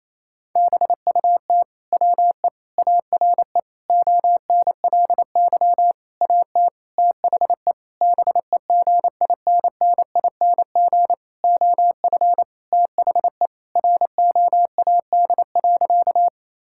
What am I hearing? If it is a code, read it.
Morse code, heard as a keyed tone, about 28 wpm, 700 Hz. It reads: BUT WE ARE ONLY AT THE BEGINNING OF THE ROAD.
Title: Sentences from Top 1000 Words - Review Part 5 - 28wpm